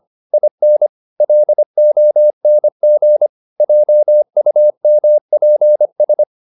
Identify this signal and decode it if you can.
Morse code, heard as a keyed tone, about 25 wpm, 600 Hz. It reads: IN LONG JUMPS
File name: the old man and the sea part 1 25wpm